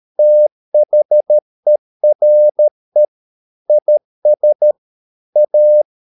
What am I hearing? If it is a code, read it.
Morse code, heard as a keyed tone, about 13 wpm, 600 Hz. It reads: THERE IS A